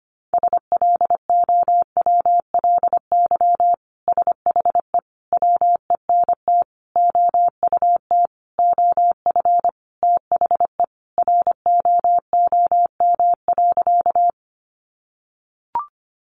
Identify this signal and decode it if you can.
Morse code, heard as a keyed tone, about 25 wpm, 700 Hz. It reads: SLOWLY SHE WENT OUT OF THE ROOM.
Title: Sentences from Top 1100 Words 25wpm